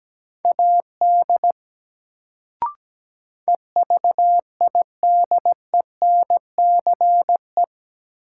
Morse code, heard as a keyed tone, about 17 wpm, 700 Hz. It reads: AD EVIDENCE
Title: Sentences from Top 1300 Words - New Word Drill - 17wpm